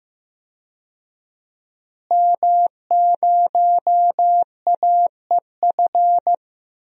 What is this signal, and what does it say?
Morse code, heard as a keyed tone, about 16 wpm, 700 Hz. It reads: M0AEF
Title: DXCC-100 Call Signs - 5 Characters - Repeated 2x - 15wpm